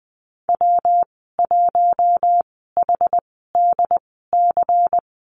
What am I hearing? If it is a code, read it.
Morse code, heard as a keyed tone, about 20 wpm, spaced as for 14 wpm, 700 Hz. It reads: W1HDC
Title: 140 Lesson 23g - Letter f - Call Signs 20wpm